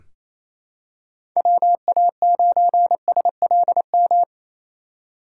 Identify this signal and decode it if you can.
Morse code, heard as a keyed tone, about 27 wpm, 700 Hz. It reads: WA9SLM
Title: USA Call Signs - 6 Characters - 6x Character Spacing - 28wpm